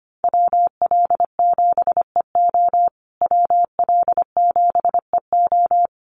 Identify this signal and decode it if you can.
Morse code, heard as a keyed tone, about 25 wpm, 700 Hz. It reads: WL7EO WL7EO
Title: USA Call Signs Repeated 3x 25wpm